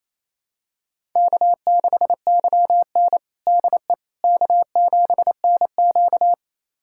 Morse code, heard as a keyed tone, about 28 wpm, 700 Hz. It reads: K6YN DE K7NQ
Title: USA Callsign DE Callsign - 4 Characters - 28wpm